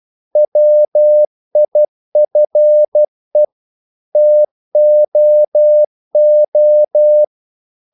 Morse code, heard as a keyed tone, about 12 wpm, 600 Hz. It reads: WIFE TOO
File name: CW / Practice / EnglishQuotations0312